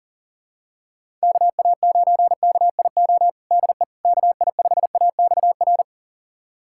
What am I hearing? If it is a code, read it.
Morse code, heard as a keyed tone, about 40 wpm, 700 Hz. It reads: KA9KIO DE KI5AXR
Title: USA Callsign DE Callsign - 6 Characters - 40wpm